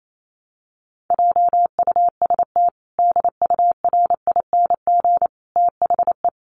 Morse code, heard as a keyed tone, about 28 wpm, 700 Hz. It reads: JUST DURING THE
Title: Sentences from Top 700 Words 28wpm